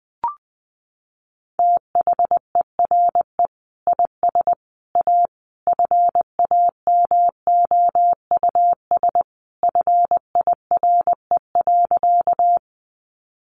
Morse code, heard as a keyed tone, about 20 wpm, 700 Hz. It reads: THERE IS A FAMOUS FIRE.
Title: Sentences from Top 1500 Words 20wpm